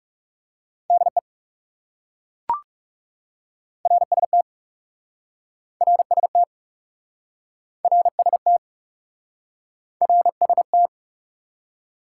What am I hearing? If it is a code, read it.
Morse code, heard as a keyed tone, about 41 wpm, 700 Hz. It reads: DE RST RST RST RST